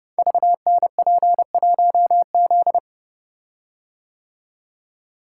Morse code, heard as a keyed tone, about 28 wpm, 700 Hz. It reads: VNP1Z
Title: Code Groups - 5 Letter:Numbers - 30wpm